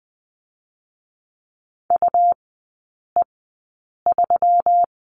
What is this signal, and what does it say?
Morse code, heard as a keyed tone, about 20 wpm, spaced as for 8 wpm, 700 Hz. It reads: UE3
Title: 212 Lesson 34e - Number 8 - Three Characters 20wpm